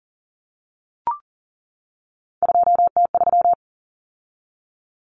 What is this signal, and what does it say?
Morse code, heard as a keyed tone, about 40 wpm, 700 Hz. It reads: 2T3
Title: Common Cut Numbers - 3-Digit 40wpm